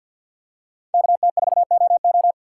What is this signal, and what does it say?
Morse code, heard as a keyed tone, about 53 wpm, 690 Hz. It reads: KT4OO
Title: USA Call Signs - 2x Character Spacing - 50wpm